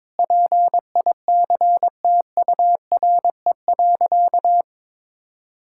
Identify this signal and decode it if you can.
Morse code, heard as a keyed tone, about 22 wpm, 700 Hz. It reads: PICTURE.
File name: practice / 22wpm / Sentences with 8-Word Phrases - 22wpm